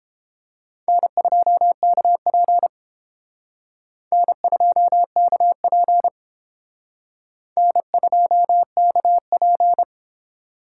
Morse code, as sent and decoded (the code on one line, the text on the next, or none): -. ..--- -.- .--. / -. ..--- -.- .--. / -. ..--- -.- .--.
N2KP N2KP N2KP